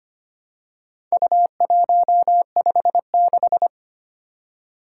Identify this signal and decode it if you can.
Morse code, heard as a keyed tone, about 25 wpm, 700 Hz. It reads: U156
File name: practice / 25wpm / Four Letter-Number Combinations - 25wpm